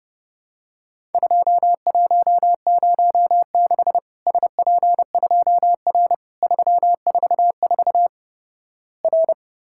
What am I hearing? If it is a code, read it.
Morse code, heard as a keyed tone, about 30 wpm, 700 Hz. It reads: 2106 SP2R 344 R
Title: Worked All Europe DX Contest - QTC Exchange - 30wpm